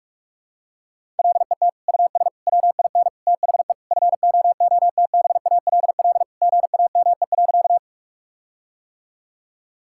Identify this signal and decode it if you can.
Morse code, heard as a keyed tone, about 46 wpm, 700 Hz. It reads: LET US WIN THE FOOTBALL GAME.
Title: Sentences from Top 1600 Words 45wpm